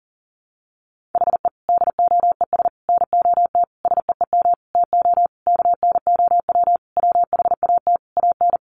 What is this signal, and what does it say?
Morse code, heard as a keyed tone, about 40 wpm, 700 Hz. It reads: HE DOES NOT SEEM TO KNOW WHAT AN
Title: Sentences from Top 1400 Words 40wpm